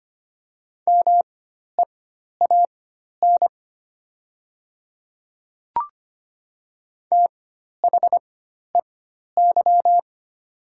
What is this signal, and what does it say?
Morse code, heard as a keyed tone, about 25 wpm, spaced as for 12 wpm, 700 Hz. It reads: MEAN THEY
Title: 4x Character Spacing - Top 200 Words - 25wpm